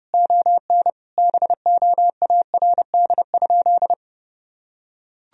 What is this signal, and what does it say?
Morse code, heard as a keyed tone, about 30 wpm, 700 Hz. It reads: ON BOARD?
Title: Sentences from Top 1000 Words - Review Part 4 - 30wpm